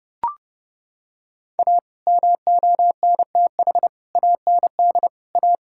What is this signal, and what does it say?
Morse code, heard as a keyed tone, about 28 wpm, 700 Hz. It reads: A MONTH AND A
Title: Sentences from Top 1000 Words - Review Part 5 - 30wpm